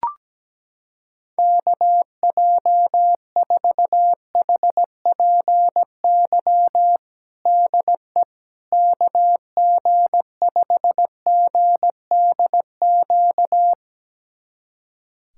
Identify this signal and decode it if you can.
Morse code, heard as a keyed tone, about 17 wpm, 700 Hz. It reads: KJ4HPY DE KG5GDQ
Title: USA Callsign DE Callsign 17wpm